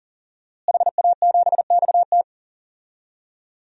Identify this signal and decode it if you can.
Morse code, heard as a keyed tone, about 40 wpm, 700 Hz. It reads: HA7XT
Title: DXCC-100 Call Signs - 5 Characters - Repeated 2x - 40wpm